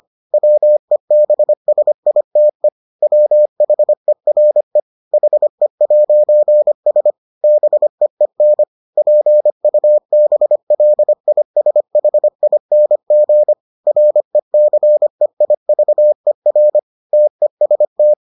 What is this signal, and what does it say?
Morse code, heard as a keyed tone, about 25 wpm, 600 Hz. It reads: WEBSITE WHERE HE'S BEEN PUBLISHING RECEIVER TEST